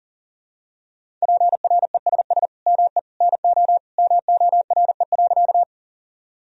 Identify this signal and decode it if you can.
Morse code, heard as a keyed tone, about 37 wpm, 700 Hz. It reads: PRESS ME NO MORE.
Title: Sentences from Top 1000 Words - Review Part 1 - 40wpm